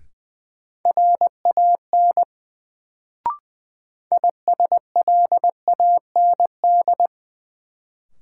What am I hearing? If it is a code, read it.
Morse code, heard as a keyed tone, about 20 wpm, 700 Hz. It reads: RAN ISLAND
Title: Sentences from Top 500 Words - New Word Drill - 20wpm